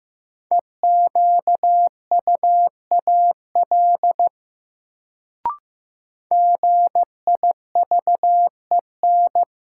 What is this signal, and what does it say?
Morse code, heard as a keyed tone, about 15 wpm, 700 Hz. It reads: EQUAL GIVEN
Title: Sentences from Top 1400 Words - New Word Drill - 15wpm